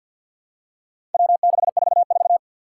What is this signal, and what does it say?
Morse code, heard as a keyed tone, about 51 wpm, 700 Hz. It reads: W644
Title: Four Challenging Letter-Number Combinations 50wpm